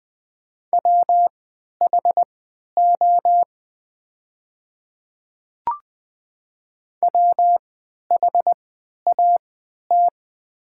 Morse code, heard as a keyed tone, about 20 wpm, spaced as for 11 wpm, 700 Hz. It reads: WHO WHAT
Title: 3x Character Spacing - Top 100 Words - 20wpm